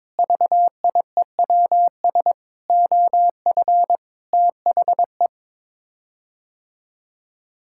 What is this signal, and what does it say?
Morse code, heard as a keyed tone, about 21 wpm, 700 Hz. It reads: VIEWS OF THE
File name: practice / 22wpm / Sets of 8 Words - 22wpm